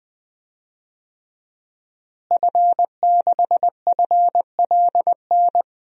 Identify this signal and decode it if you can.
Morse code, heard as a keyed tone, about 21 wpm, 700 Hz. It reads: F6FLN
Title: DXCC-100 Call Signs - 5 Characters - Repeated 3x - 20wpm